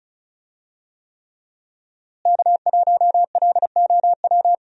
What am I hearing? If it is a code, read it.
Morse code, heard as a keyed tone, about 36 wpm, 700 Hz. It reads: K1LOW
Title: USA Call Signs - 5 Characters - Repeated 3x - 35wpm